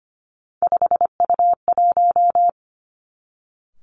Morse code, heard as a keyed tone, about 25 wpm, 700 Hz. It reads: 5U1
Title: Three Challenging Letter-Number Combinations 25wpm